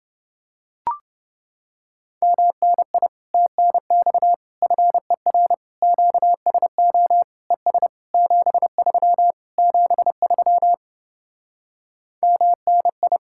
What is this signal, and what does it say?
Morse code, heard as a keyed tone, about 30 wpm, 700 Hz. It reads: MNI TNX FER QSO ES 73 73 MNI